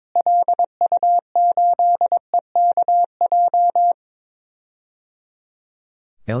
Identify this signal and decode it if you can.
Morse code, heard as a keyed tone, about 22 wpm, 700 Hz. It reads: LU8EKJ
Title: DXCC-100 Call Signs - 6 Characters - Repeated 3x - 22wpm